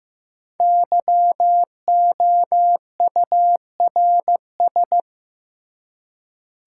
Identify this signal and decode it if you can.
Morse code, heard as a keyed tone, about 15 wpm, 700 Hz. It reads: YOURS
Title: Five Letter Words - 15wpm